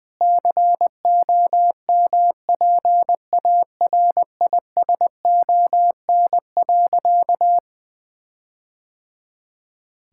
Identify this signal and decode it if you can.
Morse code, heard as a keyed tone, about 20 wpm, 700 Hz. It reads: COMPARISON.
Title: Sentences from Top 1600 Words 20wpm